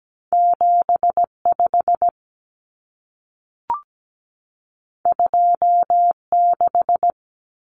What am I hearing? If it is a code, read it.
Morse code, heard as a keyed tone, about 17 wpm, 700 Hz. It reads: 75 26
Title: Numbers - 2-Digit 17wpm